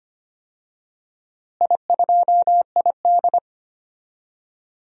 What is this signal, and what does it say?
Morse code, heard as a keyed tone, about 25 wpm, 700 Hz. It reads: I2ID
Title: DXCC-100 Call Signs - 4 Characters - Repeated 3x - 25wpm